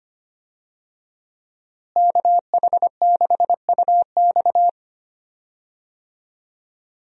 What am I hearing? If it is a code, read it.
Morse code, heard as a keyed tone, about 25 wpm, 700 Hz. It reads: KH6UX
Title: USA Call Signs - 5 Characters - Repeated 3x - 25wpm